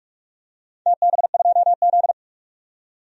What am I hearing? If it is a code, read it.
Morse code, heard as a keyed tone, about 46 wpm, 700 Hz. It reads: TB2Z
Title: DXCC-100 Call Signs - 4 Characters - 45wpm